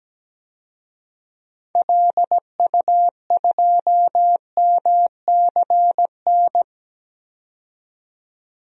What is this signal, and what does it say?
Morse code, heard as a keyed tone, about 17 wpm, 700 Hz. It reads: LU2MCN
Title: DXCC-100 Call Signs - 6 Characters - 17wpm